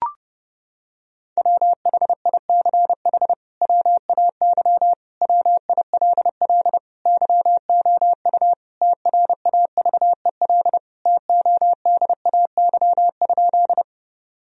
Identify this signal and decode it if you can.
Morse code, heard as a keyed tone, about 31 wpm, 700 Hz. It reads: WHICH WAY WILL YOU TRAVEL TODAY?
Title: Sentences from Top 1000 Words - Review Part 2 - 30wpm